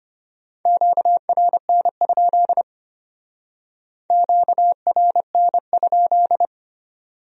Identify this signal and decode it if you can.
Morse code, heard as a keyed tone, about 33 wpm, 700 Hz. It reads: QRN? QRN?